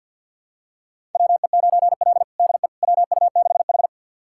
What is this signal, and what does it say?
Morse code, heard as a keyed tone, about 51 wpm, 700 Hz. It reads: WE9L DE WU6H
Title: USA Callsign DE Callsign - 4 Characters - 50wpm